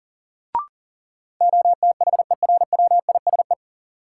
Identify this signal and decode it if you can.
Morse code, heard as a keyed tone, about 40 wpm, 700 Hz. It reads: OTHERWISE
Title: Sentences from Top 1200 Words - New Word Drill - 40wpm